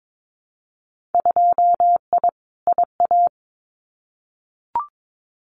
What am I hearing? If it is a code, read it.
Morse code, heard as a keyed tone, about 22 wpm, 700 Hz. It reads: 2I IA
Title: Winter Field Day Exchanges Encore 22wpm